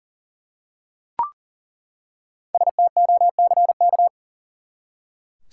Morse code, heard as a keyed tone, about 40 wpm, 700 Hz. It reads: STOCK
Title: Five Letter Words - 40wpm